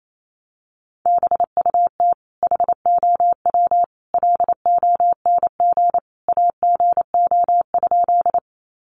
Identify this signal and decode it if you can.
Morse code, heard as a keyed tone, about 28 wpm, 700 Hz. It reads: BUT HOW LONG AGO?
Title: Sentences from Top 500 Words 28wpm